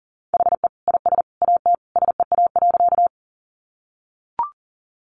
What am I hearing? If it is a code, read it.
Morse code, heard as a keyed tone, about 40 wpm, 700 Hz. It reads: HE IS AT SEA.
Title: Sentences from Top 300 Words 40wpm